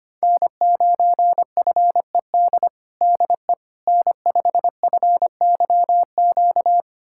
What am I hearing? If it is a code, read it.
Morse code, heard as a keyed tone, about 25 wpm, 700 Hz. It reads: N9FED DE N5FYQ
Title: USA Callsign DE Callsign - 5 Characters - 25wpm